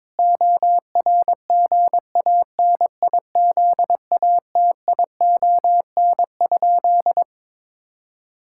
Morse code, heard as a keyed tone, about 22 wpm, 700 Hz. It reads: ORGANIZATION?
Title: Sentences from Top 1000 Words - Review Part 5 - 22wpm